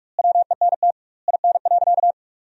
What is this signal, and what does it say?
Morse code, heard as a keyed tone, about 41 wpm, 700 Hz. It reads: WENT IN.
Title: 2x Word-Spacing - Sentences from Top 300 Words - 45wpm